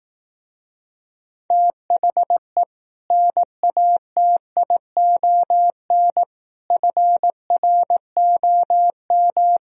Morse code, heard as a keyed tone, about 18 wpm, 700 Hz. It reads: THE NATION FROM
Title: Speed-Racing - Sets of 5 Words 15wpm